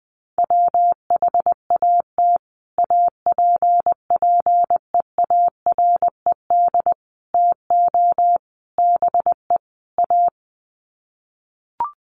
Morse code, heard as a keyed tone, about 20 wpm, 700 Hz. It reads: WHAT APPEARED TO BE A
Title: Sets of 5 Words Encore2 20wpm